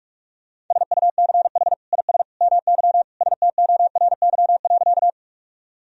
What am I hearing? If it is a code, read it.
Morse code, heard as a keyed tone, about 46 wpm, 700 Hz. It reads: SUCH IS MY STORY.
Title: Sentences from Top 1000 Words - Review Part 1 - 45wpm